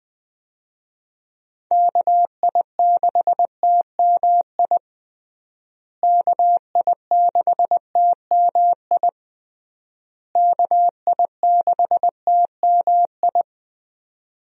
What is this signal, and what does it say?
Morse code, heard as a keyed tone, about 20 wpm, 700 Hz. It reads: KI6TMI KI6TMI KI6TMI
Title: USA Call Signs - 6 Characters - Repeated 3x - 20wpm